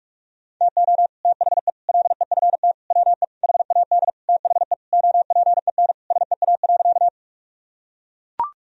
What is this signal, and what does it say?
Morse code, heard as a keyed tone, about 41 wpm, 700 Hz. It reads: TO THE LEFT WE HAD THE OPEN SEA.
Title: Sentences from Top 1000 Words - Review Part 1 - 45wpm